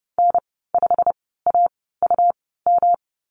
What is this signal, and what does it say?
Morse code, heard as a keyed tone, about 30 wpm, spaced as for 17 wpm, 700 Hz. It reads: N5AUM
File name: Lesson 19g - Letter w - Call Signs 30wpm